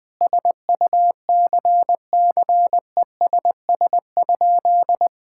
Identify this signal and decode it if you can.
Morse code, heard as a keyed tone, about 20 wpm, 700 Hz. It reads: SUCCESS?
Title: Sentences from Top 1000 Words - Review Part 6 - 20wpm